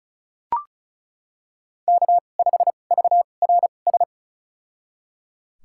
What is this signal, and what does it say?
Morse code, heard as a keyed tone, about 36 wpm, spaced as for 25 wpm, 700 Hz. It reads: K5VRS
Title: USA Call Signs - 2x Character Spacing - 35wpm